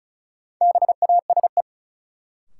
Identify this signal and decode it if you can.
Morse code, heard as a keyed tone, about 36 wpm, 700 Hz. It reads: BASE
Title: Sentences from Top 1600 Words - New Word Drill Alt - 35wpm